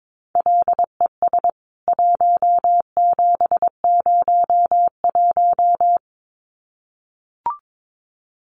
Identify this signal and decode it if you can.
Morse code, heard as a keyed tone, about 22 wpm, 700 Hz. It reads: LES 1701
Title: CWT Exchange 22wpm